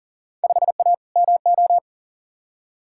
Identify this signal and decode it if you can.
Morse code, heard as a keyed tone, about 37 wpm, 700 Hz. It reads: 5A MO
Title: Field Day Exchanges - 40wpm